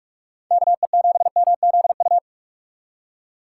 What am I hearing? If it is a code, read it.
Morse code, heard as a keyed tone, about 46 wpm, 700 Hz. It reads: KE7MZU